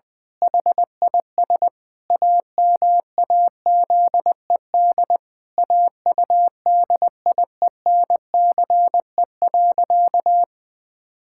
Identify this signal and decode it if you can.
Morse code, heard as a keyed tone, about 20 wpm, 700 Hz. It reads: HIS AMAZED AUDIENCE.
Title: Pangram Challenge v2 - 20wpm